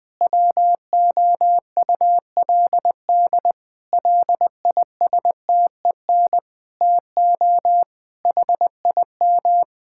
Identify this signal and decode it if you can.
Morse code, heard as a keyed tone, about 20 wpm, 700 Hz. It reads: WOULD LISTEN TO HIM
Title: Sentences from Top 1600 Words 20wpm